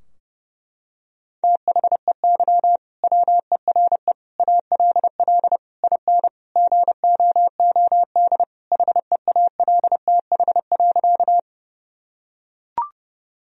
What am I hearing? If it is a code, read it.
Morse code, heard as a keyed tone, about 30 wpm, 700 Hz. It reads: THEY WERE ALL IN GOOD HEALTH.
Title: Sentences from Top 1000 Words - Review Part 3 - 30wpm